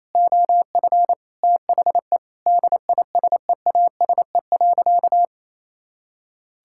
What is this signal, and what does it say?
Morse code, heard as a keyed tone, about 28 wpm, 700 Hz. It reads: OF THE DISEASE.
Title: Sentences from Top 1000 Words - Review Part 6 - 28wpm